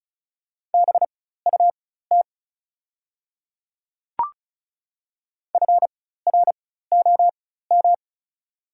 Morse code, heard as a keyed tone, about 36 wpm, spaced as for 16 wpm, 700 Hz. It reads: BUT FROM
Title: 4x Character Spacing - Top 100 Words - 35wpm